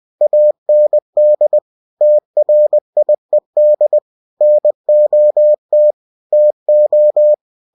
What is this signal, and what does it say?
Morse code, heard as a keyed tone, about 20 wpm, 600 Hz. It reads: AND TRIED NOT TO
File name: cw-books / winnie-the-pooh / winnie-20wpm-page-0063